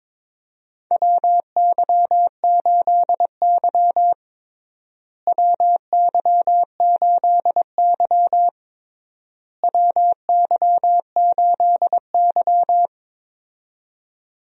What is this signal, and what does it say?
Morse code, heard as a keyed tone, about 22 wpm, 700 Hz. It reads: WY8Y WY8Y WY8Y
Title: USA Call Signs - 4 Characters - Repeated 3x - 22wpm